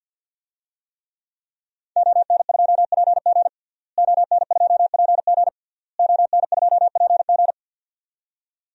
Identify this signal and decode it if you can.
Morse code, heard as a keyed tone, about 51 wpm, 700 Hz. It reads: ON2PG ON2PG ON2PG
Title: DXCC-100 Call Signs - 5 Characters - Repeated 3x - 50wpm